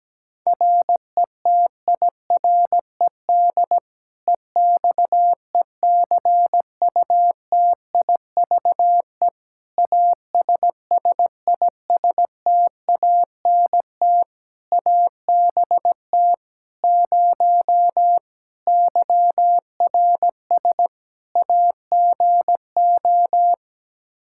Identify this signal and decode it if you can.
Morse code, heard as a keyed tone, about 17 wpm, 700 Hz. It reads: RETIRED EXECUTIVE ASSISTANT ABT 0 YRS AGO
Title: K1USN SST QSO Practice 17wpm